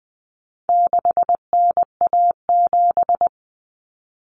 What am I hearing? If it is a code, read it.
Morse code, heard as a keyed tone, about 20 wpm, 700 Hz. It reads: 6NA7
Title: Common Cut Numbers - 4-Digit 20wpm